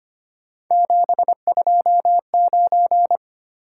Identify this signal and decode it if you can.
Morse code, heard as a keyed tone, about 25 wpm, 700 Hz. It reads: 729
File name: Three Letter-Number Combinations - 25wpm